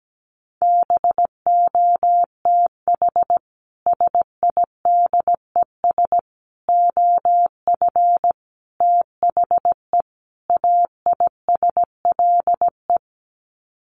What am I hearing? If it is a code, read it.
Morse code, heard as a keyed tone, about 17 wpm, 700 Hz. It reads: BOTH SIDES OF THE AISLE
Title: Sets of 5 Words Encore2 17wpm